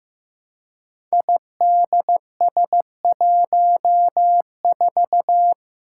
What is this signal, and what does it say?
Morse code, heard as a keyed tone, about 15 wpm, 700 Hz. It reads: IDS14
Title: Five Challenging Letter-Number Combinations 15wpm